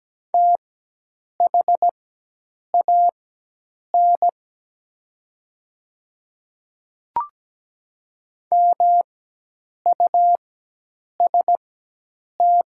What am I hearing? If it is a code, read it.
Morse code, heard as a keyed tone, about 17 wpm, spaced as for 8 wpm, 700 Hz. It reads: THAN MUST
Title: 4x Character Spacing - Top 200 Words - 17wpm